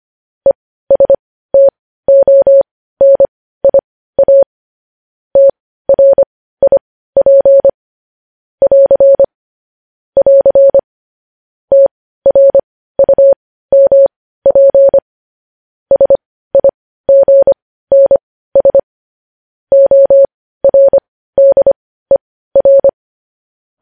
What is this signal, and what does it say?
Morse code, heard as a keyed tone, about 25 wpm, spaced as for 15 wpm, 550 Hz. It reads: ESTONIA TRIP <AR> <AR> TRUMP SIGNS ORDER
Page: news-headlines-in-morse-code-at-15-wpm